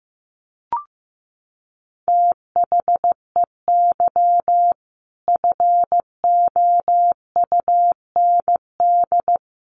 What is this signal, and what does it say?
Morse code, heard as a keyed tone, about 15 wpm, 700 Hz. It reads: THEY FOUND
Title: Sentences from Top 1500 Words 15wpm